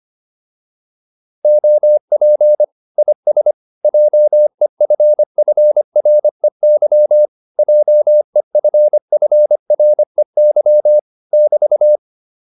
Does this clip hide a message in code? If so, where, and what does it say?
Yes: Morse code from 1.4 to 12.0 s, heard as a keyed tone, about 25 wpm, 600 Hz. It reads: OP IS JEFFREY JEFFREY <BT>